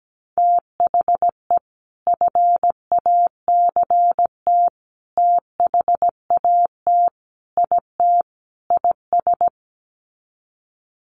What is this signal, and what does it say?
Morse code, heard as a keyed tone, about 17 wpm, 700 Hz. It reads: THE FACT THAT IT IS
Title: Sets of 5 Words Encore2 17wpm